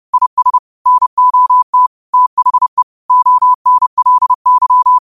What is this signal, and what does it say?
Morse code, heard as a keyed tone, about 30 wpm, 1000 Hz. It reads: IS NOT THE ONLY
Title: 30 WPM morse code quotes for Thu, 29 Feb 2024